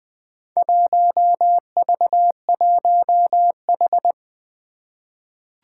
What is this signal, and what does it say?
Morse code, heard as a keyed tone, about 20 wpm, 700 Hz. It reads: 1V1H
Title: Four Challenging Letter-Number Combinations 20wpm